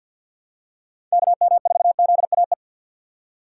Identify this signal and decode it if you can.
Morse code, heard as a keyed tone, about 52 wpm, 700 Hz. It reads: KM4ZAE
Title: USA Call Signs - 2x Character Spacing - 50wpm